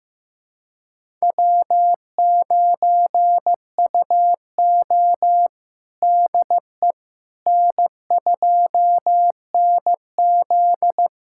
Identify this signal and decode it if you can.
Morse code, heard as a keyed tone, about 15 wpm, 700 Hz. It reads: W9UO DE N2NZ
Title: USA Callsign DE Callsign - 4 Characters - 15wpm